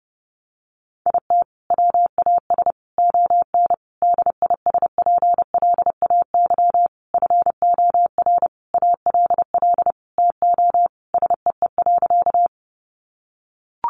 Morse code, heard as a keyed tone, about 30 wpm, 700 Hz. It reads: IT WAS ON DISPLAY FOR ALL TO SEE.
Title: Sentences from Top 1200 Words 30wpm